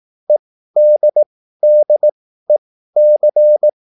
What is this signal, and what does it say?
Morse code, heard as a keyed tone, about 18 wpm, spaced as for 13 wpm, 600 Hz. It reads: EDDEC
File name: QSL / CW lesson